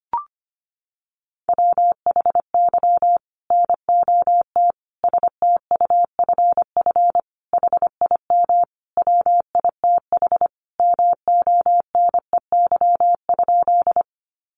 Morse code, heard as a keyed tone, about 25 wpm, 700 Hz. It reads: WHY NOT STUFF HIM WITH MONEY?
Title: Sentences from Top 1400 Words 25wpm